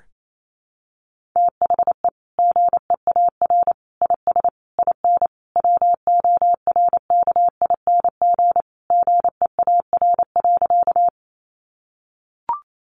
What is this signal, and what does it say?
Morse code, heard as a keyed tone, about 28 wpm, 700 Hz. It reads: THE GEAR IS IN WORKING GEAR.
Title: Sentences from Top 1400 Words 28wpm